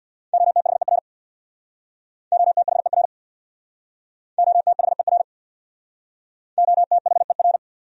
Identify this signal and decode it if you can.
Morse code, heard as a keyed tone, about 69 wpm, 700 Hz. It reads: OTHER OTHER OTHER OTHER